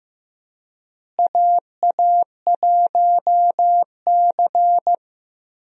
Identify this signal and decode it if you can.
Morse code, heard as a keyed tone, about 15 wpm, 700 Hz. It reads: AA1C
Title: USA Call Signs - 4 Characters - Repeated 2x - 15wpm